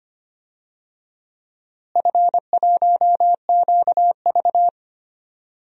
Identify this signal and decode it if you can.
Morse code, heard as a keyed tone, about 25 wpm, 700 Hz. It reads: F1QV
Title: DXCC-100 Call Signs - 4 Characters - Repeated 3x - 25wpm